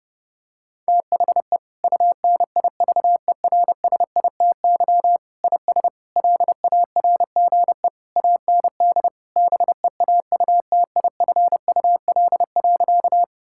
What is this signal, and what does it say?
Morse code, heard as a keyed tone, about 32 wpm, 700 Hz. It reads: THE UNIVERSITY IS LARGE AND BEAUTIFUL.
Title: Sentences from Top 1000 Words - Review Part 5 - 30wpm